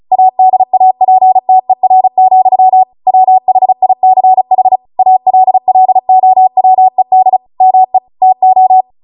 Morse code, heard as a keyed tone, about 36 wpm, 750 Hz. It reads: ADAPTER, WHICH ALLOWED ME TO